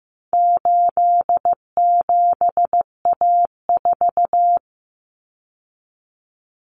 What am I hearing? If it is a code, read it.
Morse code, heard as a keyed tone, about 15 wpm, 700 Hz. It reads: 87A4
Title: Common Cut Numbers - 4-Digit 15wpm